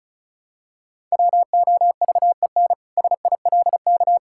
Morse code, heard as a keyed tone, about 34 wpm, 700 Hz. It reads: WOVEN SILK
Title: Pangram Challenge v2 - 35wpm